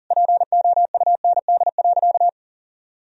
Morse code, heard as a keyed tone, about 40 wpm, 700 Hz. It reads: POUND.
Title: Sentences from Top 1000 Words - Review Part 2 - 40wpm